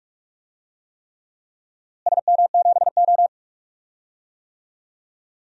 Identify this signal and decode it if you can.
Morse code, heard as a keyed tone, about 46 wpm, 700 Hz. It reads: SM7O
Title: DXCC-100 Call Signs - 4 Characters - Repeated 3x - 45wpm